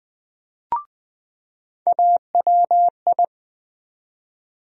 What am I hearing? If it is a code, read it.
Morse code, heard as a keyed tone, about 20 wpm, 700 Hz. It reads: AWI
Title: USA Call Sign Postfixes - 3 Letters 20wpm